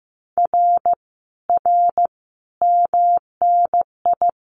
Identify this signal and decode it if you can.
Morse code, heard as a keyed tone, about 15 wpm, 700 Hz. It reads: R R MNI
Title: Common QSO Phrases Encore 15wpm